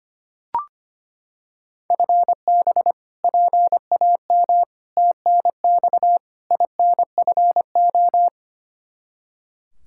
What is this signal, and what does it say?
Morse code, heard as a keyed tone, about 25 wpm, 700 Hz. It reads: FB PAM TNX INFO
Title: Practice QSO - Exchange 3 - More Info - 25wpm